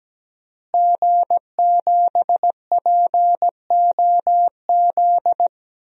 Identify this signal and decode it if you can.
Morse code, heard as a keyed tone, about 17 wpm, 700 Hz. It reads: G7POZ
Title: DXCC-100 Call Signs - 5 Characters - Repeated 3x - 17wpm